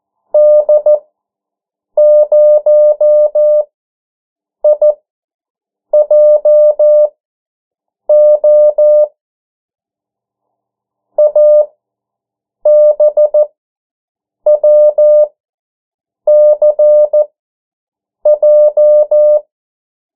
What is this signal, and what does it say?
Morse code, heard as a keyed tone, about 14 wpm, spaced as for 7 wpm, 600 Hz. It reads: D0IJO ABWCJ
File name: morse mp3 / 09 urok 8